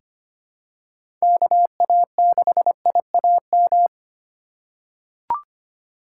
Morse code, heard as a keyed tone, about 25 wpm, 700 Hz. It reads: KA6IAM
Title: USA Call Signs - 6 Characters - 2x Character Spacing - 25wpm